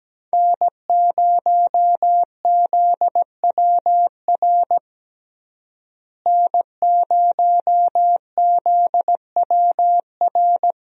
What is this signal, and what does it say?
Morse code, heard as a keyed tone, about 17 wpm, 700 Hz. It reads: N0ZWR N0ZWR
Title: USA Call Signs - 5 Characters - Repeated 3x - 17wpm